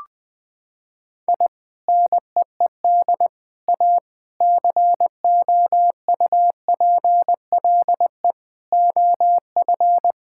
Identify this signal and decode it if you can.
Morse code, heard as a keyed tone, about 20 wpm, 700 Hz. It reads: I NEED A COUPLE OF
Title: Sentences with 3-Word Phrases - 20wpm